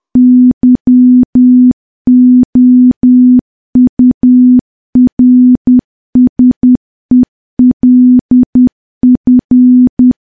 Morse code, heard as a keyed tone, about 10 wpm, 250 Hz. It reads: YOURSELF
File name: morse / morse-training-10WPM